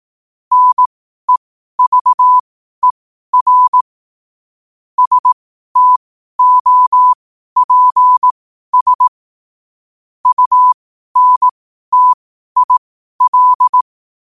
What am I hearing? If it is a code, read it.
Morse code, heard as a keyed tone, about 18 wpm, spaced as for 12 wpm, 1000 Hz. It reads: NEVER STOPS UNTIL
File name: qotdfiles / 12wpm